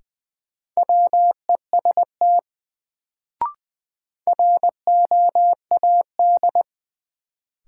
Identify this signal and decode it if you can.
Morse code, heard as a keyed tone, about 20 wpm, 700 Hz. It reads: WEST ROAD
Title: Sets of 5 Words - New Word Drill - 20wpm